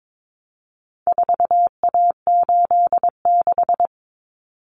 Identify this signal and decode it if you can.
Morse code, heard as a keyed tone, about 22 wpm, 700 Hz. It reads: 4A86
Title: Common Cut Numbers - 4-Digit 22wpm